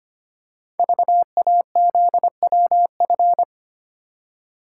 Morse code, heard as a keyed tone, about 25 wpm, 700 Hz. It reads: VAZWF
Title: Code Groups - 5 Letter:Numbers - 25wpm